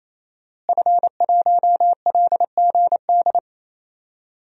Morse code, heard as a keyed tone, about 27 wpm, 700 Hz. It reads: F1LGD
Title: DXCC-100 Call Signs - 5 Characters - Repeated 3x - 28wpm